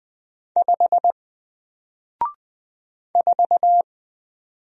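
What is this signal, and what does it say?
Morse code, heard as a keyed tone, about 20 wpm, 700 Hz. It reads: 5 4
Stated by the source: Single letter and number Morse drill at 20 wpm, from N4LCD.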